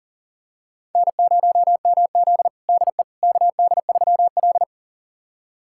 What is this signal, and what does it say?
Morse code, heard as a keyed tone, about 40 wpm, 700 Hz. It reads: N0MZ DE KD3L
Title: USA Callsign DE Callsign - 4 Characters - 40wpm